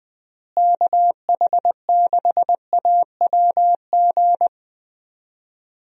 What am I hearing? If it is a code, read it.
Morse code, heard as a keyed tone, about 20 wpm, 700 Hz. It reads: KH6AWG
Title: DXCC-100 Call Signs - 6 Characters - Repeated 2x - 20wpm